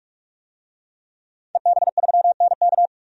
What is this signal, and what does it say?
Morse code, heard as a keyed tone, about 42 wpm, 700 Hz. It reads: EB3NK
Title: DXCC-100 Call Signs - 5 Characters - 45wpm